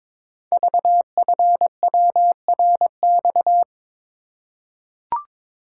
Morse code, heard as a keyed tone, about 22 wpm, 690 Hz. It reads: VFWRX
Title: Code Groups - 5 Letters - 22wpm